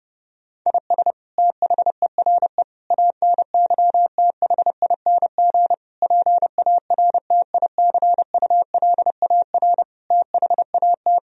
Morse code, heard as a keyed tone, about 30 wpm, 700 Hz. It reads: IS THERE ANYTHING PARTICULAR THAT
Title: Sentences from Top 1000 Words - Review Part 4 - 30wpm